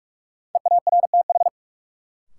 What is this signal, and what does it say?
Morse code, heard as a keyed tone, about 46 wpm, 700 Hz. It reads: EARTH
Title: Sentences from Top 200 Words - New Word Drill - 45wpm